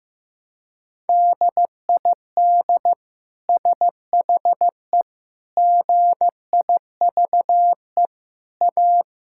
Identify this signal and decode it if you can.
Morse code, heard as a keyed tone, about 15 wpm, 700 Hz. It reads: DID SHE GIVE A
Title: Sentences from Top 1000 Words - Review Part 6 - 15wpm